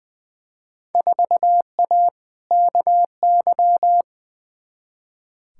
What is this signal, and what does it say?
Morse code, heard as a keyed tone, about 20 wpm, 700 Hz. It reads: 4A KY
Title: Field Day Exchanges Encore - 20wpm